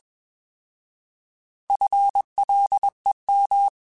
Morse code, heard as a keyed tone, about 21 wpm, 800 Hz. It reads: FLEM
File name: Alphabetical / Alphabetical mp3s / Fs @20